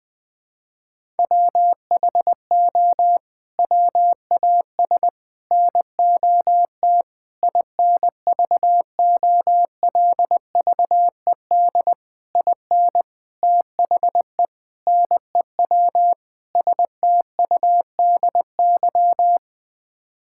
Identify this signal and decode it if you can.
Morse code, heard as a keyed tone, about 21 wpm, 700 Hz. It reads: WHO WAS NOT INVOLVED IN THE NEW STUDY
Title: Sets of 8 Words - 20wpm